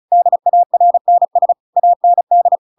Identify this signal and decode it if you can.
Morse code, heard as a keyed tone, about 34 wpm, 700 Hz. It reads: DARNS AND
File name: Pinocchio / Pinocchio ch8 35 35 ik1hge